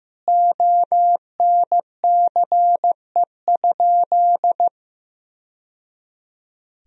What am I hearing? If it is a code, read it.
Morse code, heard as a keyed tone, about 16 wpm, 700 Hz. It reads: ONCE?
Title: Sentences from Top 1000 Words - Review Part 1 - 15wpm